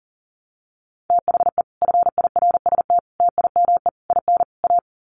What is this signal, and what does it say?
Morse code, heard as a keyed tone, about 40 wpm, 700 Hz. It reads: THE FIRST TIME IN A
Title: Sets of 5 Words Encore 40wpm